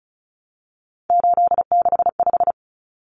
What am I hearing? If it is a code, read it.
Morse code, heard as a keyed tone, about 34 wpm, 700 Hz. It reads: 865
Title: Common Cut Numbers - 3-Digit 35wpm